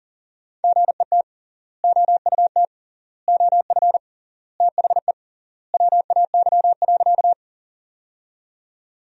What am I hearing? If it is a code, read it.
Morse code, heard as a keyed tone, about 40 wpm, 700 Hz. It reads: GET OUT OF THE WAY.
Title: 3x Word-Spacing - Sentences from Top 200 Words - 40wpm